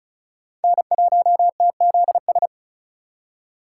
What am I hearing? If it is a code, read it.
Morse code, heard as a keyed tone, about 46 wpm, 700 Hz. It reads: N1TZS